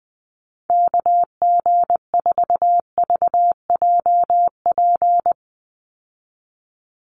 Morse code, heard as a keyed tone, about 20 wpm, 700 Hz. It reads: KG4VJP
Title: USA Call Signs 20wpm